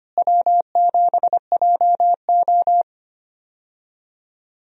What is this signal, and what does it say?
Morse code, heard as a keyed tone, about 25 wpm, 700 Hz. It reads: W7JO
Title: USA Call Signs - 4 Characters - Repeated 2x - 25wpm